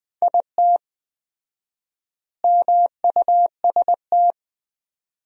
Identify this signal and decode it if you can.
Morse code, heard as a keyed tone, about 20 wpm, 700 Hz. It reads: IT MUST
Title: 4x Word-Spacing - Sentences from Top 200 Words - 20wpm